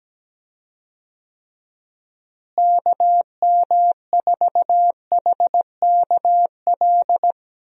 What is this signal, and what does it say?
Morse code, heard as a keyed tone, about 17 wpm, 700 Hz. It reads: KM4HKL
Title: USA Call Signs - 6 Characters - Repeated 2x - 17wpm